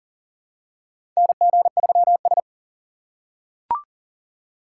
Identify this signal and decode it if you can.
Morse code, heard as a keyed tone, about 40 wpm, 700 Hz. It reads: NG3S
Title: USA Call Signs - 6x Character Spacing - 40wpm